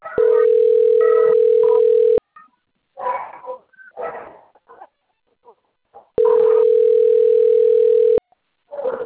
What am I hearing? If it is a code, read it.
Ringback tone (North America), 440 + 480 Hz.
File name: out- - - - - .6382